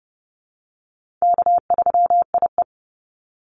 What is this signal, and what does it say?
Morse code, heard as a keyed tone, about 30 wpm, 700 Hz. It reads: K3IE
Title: Call Signs - 30wpm